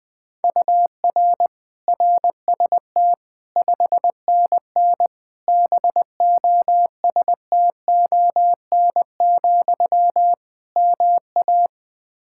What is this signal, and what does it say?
Morse code, heard as a keyed tone, about 20 wpm, 700 Hz. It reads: UR RST 5NN BOSTON, MA
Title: Practice QSO - Exchange 1 - Beginning - 20wpm